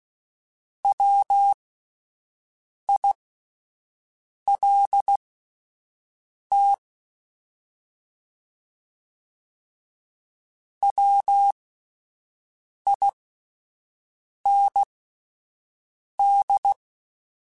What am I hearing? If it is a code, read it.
Morse code, heard as a keyed tone, about 16 wpm, spaced as for 5 wpm, 800 Hz. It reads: WILT WIND
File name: Alphabetical / Alphabetical mp3s / Ws @5